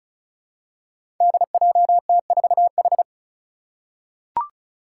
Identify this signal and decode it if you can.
Morse code, heard as a keyed tone, about 36 wpm, 700 Hz. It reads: DJT4H
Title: Five Challenging Letter-Number Combinations 35wpm